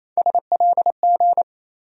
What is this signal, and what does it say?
Morse code, heard as a keyed tone, about 28 wpm, 700 Hz. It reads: SLG
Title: USA Call Sign Postfixes - 3 Letters 28wpm